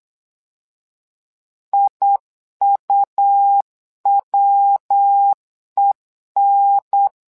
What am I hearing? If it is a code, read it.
Morse code, heard as a keyed tone, about 8 wpm, 800 Hz. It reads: IUWEN
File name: LEZIONI / 3 MWRO / 3 MWRO 8wpm 2606